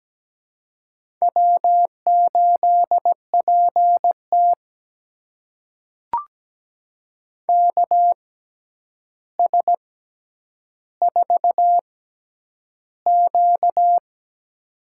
Morse code, heard as a keyed tone, about 17 wpm, spaced as for 9 wpm, 700 Hz. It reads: W8PT KS4Q
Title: USA Call Signs - 4 Characters - 6x Character Spacing - 17wpm